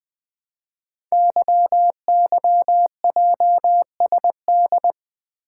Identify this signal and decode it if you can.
Morse code, heard as a keyed tone, about 20 wpm, 700 Hz. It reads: YYJSD
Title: Code Groups - 5 Letters - 20wpm